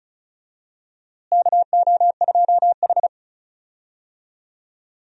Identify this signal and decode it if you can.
Morse code, heard as a keyed tone, about 36 wpm, 700 Hz. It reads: KO2H
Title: USA Call Signs - 4 Characters - Repeated 3x - 35wpm